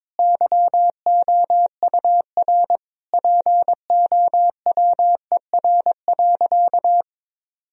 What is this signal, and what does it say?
Morse code, heard as a keyed tone, about 22 wpm, 700 Hz. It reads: YOUR POWER.
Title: Sentences from Top 1000 Words - Review Part 2 - 22wpm